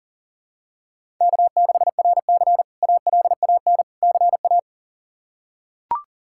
Morse code, heard as a keyed tone, about 40 wpm, 700 Hz. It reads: K6RC ALAN CA
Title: K1USN SST QSO Practice - 40wpm